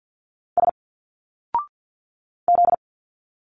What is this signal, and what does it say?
Morse code, heard as a keyed tone, about 51 wpm, 700 Hz. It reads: S Z
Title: Single Letters 50wpm